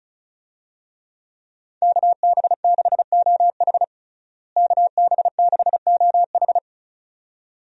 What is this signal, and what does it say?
Morse code, heard as a keyed tone, about 34 wpm, 700 Hz. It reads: KB6OH KB6OH
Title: USA Call Signs - 5 Characters - Repeated 2x - 35wpm